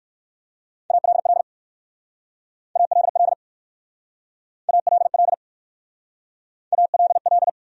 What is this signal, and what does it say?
Morse code, heard as a keyed tone, about 63 wpm, 700 Hz. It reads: ALL ALL ALL ALL